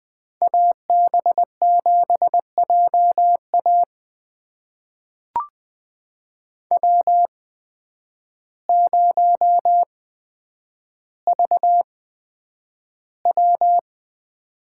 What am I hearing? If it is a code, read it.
Morse code, heard as a keyed tone, about 20 wpm, spaced as for 9 wpm, 700 Hz. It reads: AB7JA W0VW
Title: USA Call Signs - 8x Character Spacing - 20wpm